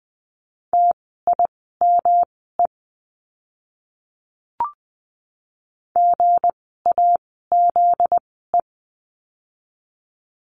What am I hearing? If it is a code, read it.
Morse code, heard as a keyed tone, about 20 wpm, spaced as for 15 wpm, 700 Hz. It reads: TIME GAZE
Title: 244 Lesson 39f - Letter z - Words 20wpm